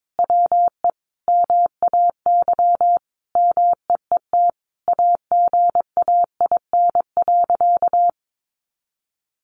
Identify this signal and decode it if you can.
Morse code, heard as a keyed tone, about 22 wpm, 700 Hz. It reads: WE MAY MEET AGAIN.
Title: Sentences from Top 1100 Words 22wpm